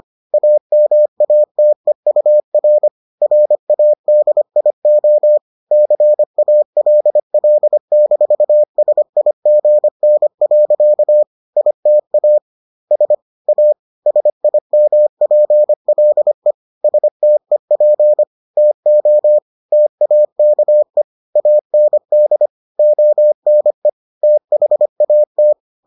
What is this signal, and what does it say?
Morse code, heard as a keyed tone, about 25 wpm, 600 Hz. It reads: AMATEUR RADIO CALL-SIGN. ITA S A SIMPLE STEP TO TAKE AND ONE THAT